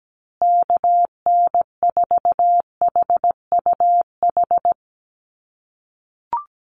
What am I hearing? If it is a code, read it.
Morse code, heard as a keyed tone, about 17 wpm, 700 Hz. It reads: KN4HUH
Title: USA Call Signs Repeated 3x 17wpm